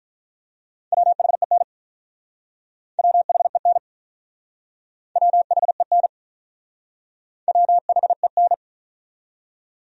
Morse code, heard as a keyed tone, about 44 wpm, 700 Hz. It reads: WHEN WHEN WHEN WHEN